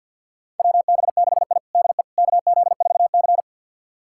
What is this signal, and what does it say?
Morse code, heard as a keyed tone, about 46 wpm, 700 Hz. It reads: WB6I DE KZ4C
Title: USA Callsign DE Callsign - 4 Characters - 50wpm